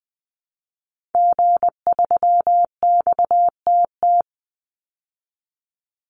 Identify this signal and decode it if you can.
Morse code, heard as a keyed tone, about 20 wpm, 700 Hz. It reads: G3XTT
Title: Call Signs Repeated 2x - 20wpm